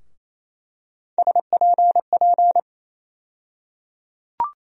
Morse code, heard as a keyed tone, about 28 wpm, 700 Hz. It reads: SPP
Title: USA Call Sign Postfixes - 3 Letters 28wpm